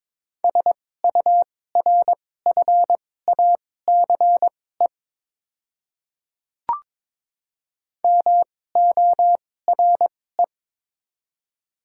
Morse code, heard as a keyed tone, about 22 wpm, spaced as for 16 wpm, 700 Hz. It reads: SURFACE MORE
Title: 2x Character Spacing - Top 500 Words - 22wpm